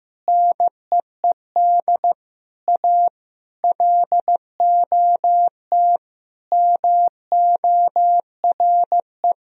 Morse code, heard as a keyed tone, about 16 wpm, 700 Hz. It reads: NEED A LOT MORE
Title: Sentences from Top 1000 Words - Review Part 3 - 15wpm